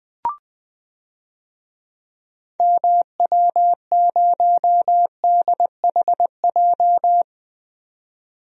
Morse code, heard as a keyed tone, about 20 wpm, 700 Hz. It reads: MW0DHJ
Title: DXCC-100 Call Signs - 6 Characters - 20wpm